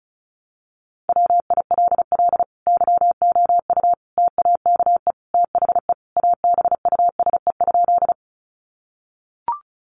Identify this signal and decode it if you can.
Morse code, heard as a keyed tone, about 34 wpm, 700 Hz. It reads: WILL YOU TAKE THE ABUSE?
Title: Sentences from Top 1300 Words 35wpm